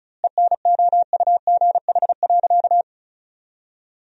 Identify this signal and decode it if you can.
Morse code, heard as a keyed tone, about 36 wpm, 700 Hz. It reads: ENOUGH.
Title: 2x Word-Spacing - Sentences from Top 400 Words - 35wpm